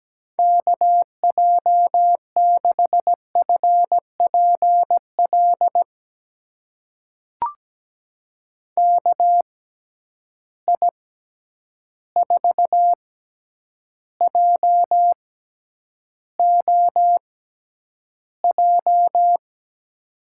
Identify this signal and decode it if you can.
Morse code, heard as a keyed tone, about 17 wpm, spaced as for 9 wpm, 700 Hz. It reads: KJ6FPL KI4JOJ
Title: USA Call Signs - 6 Characters - 6x Character Spacing - 17wpm